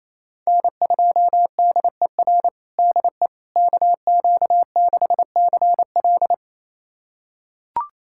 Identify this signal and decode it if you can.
Morse code, heard as a keyed tone, about 28 wpm, 700 Hz. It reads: N2DER DE KQ6CL
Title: USA Callsign DE Callsign - 5 Characters - 28wpm